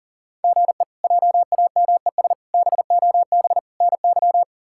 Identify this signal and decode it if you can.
Morse code, heard as a keyed tone, about 37 wpm, 700 Hz. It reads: GE JAMES BOB NY
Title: K1USN SST QSO Practice - 40wpm